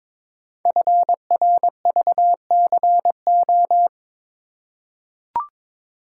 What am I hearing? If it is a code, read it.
Morse code, heard as a keyed tone, about 22 wpm, 700 Hz. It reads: FRVCO
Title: Code Groups - 5 Letters - 22wpm